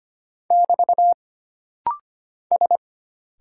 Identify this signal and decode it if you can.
Morse code, heard as a keyed tone, about 25 wpm, 700 Hz. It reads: <BT> S
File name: Single Character (Letter:Number:Punctuation:Prosign) - Rapid-Fire - 25wpm